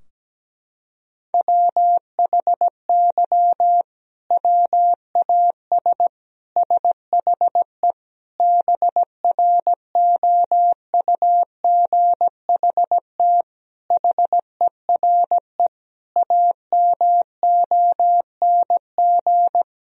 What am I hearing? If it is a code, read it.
Morse code, heard as a keyed tone, about 17 wpm, 700 Hz. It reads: WHY WAS SHE BROUGHT HERE AMONG
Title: Sentences from Top 1000 Words - Review Part 3 - 17wpm